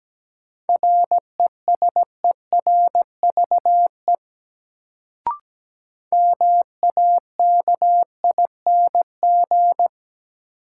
Morse code, heard as a keyed tone, about 17 wpm, 700 Hz. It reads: RESERVE MAKING
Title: Sentences from Top 1000 Words - Review Part 6 - New Word Drill - 17wpm